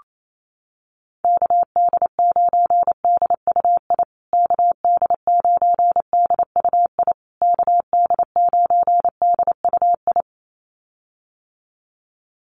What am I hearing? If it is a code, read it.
Morse code, heard as a keyed tone, about 28 wpm, 700 Hz. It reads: KD9DUI KD9DUI KD9DUI
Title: USA Call Signs Repeated 3x 28wpm